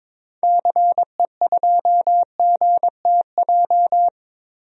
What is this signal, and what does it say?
Morse code, heard as a keyed tone, about 22 wpm, 700 Hz. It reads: CE2GTJ
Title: DXCC-100 Call Signs - 6 Characters - 22wpm